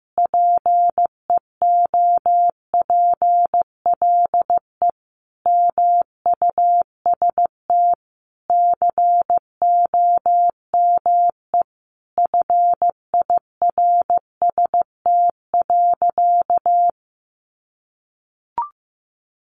Morse code, heard as a keyed tone, about 15 wpm, 700 Hz. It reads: PEOPLE MUST COME FIRST.
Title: Sentences from Top 300 Words 15wpm